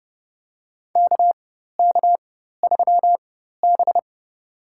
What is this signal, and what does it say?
Morse code, heard as a keyed tone, about 30 wpm, spaced as for 14 wpm, 700 Hz. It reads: KK3B
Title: USA Call Signs - 4 Characters - 4x Character Spacing - 30wpm